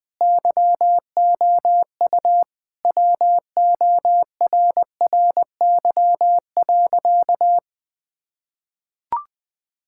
Morse code, heard as a keyed tone, about 20 wpm, 700 Hz. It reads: YOU WORRY.
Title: Sentences with 3-Word Phrases - 20wpm